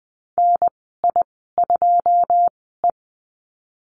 Morse code, heard as a keyed tone, about 20 wpm, spaced as for 14 wpm, 700 Hz. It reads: NI2E
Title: 167 Lesson 27g - Number 7 - Call Signs 20wpm